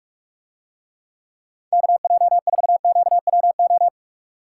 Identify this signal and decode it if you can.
Morse code, heard as a keyed tone, about 47 wpm, 700 Hz. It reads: KJ4QWO